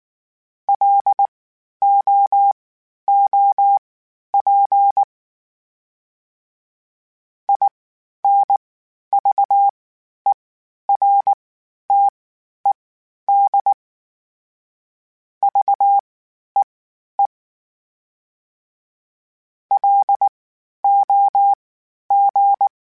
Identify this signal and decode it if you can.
Morse code, heard as a keyed tone, about 19 wpm, spaced as for 11 wpm, 800 Hz. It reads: LOOP INVERTED VEE LOG
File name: (spaced) / FromSMs spaced mp3s / Antennas sp@18